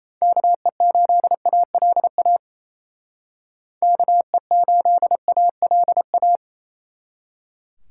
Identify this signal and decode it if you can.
Morse code, heard as a keyed tone, about 29 wpm, 700 Hz. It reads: KE8ALA KE8ALA
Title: Speed Racing - USA 2x3 Call Signs 28wpm